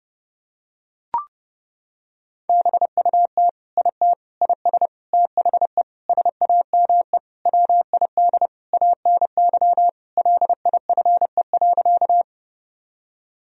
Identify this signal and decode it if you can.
Morse code, heard as a keyed tone, about 31 wpm, 700 Hz. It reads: BUT IT IS THE SAME WID ANY LIFE.
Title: Sentences from Top 300 Words and Abbreviations 30wpm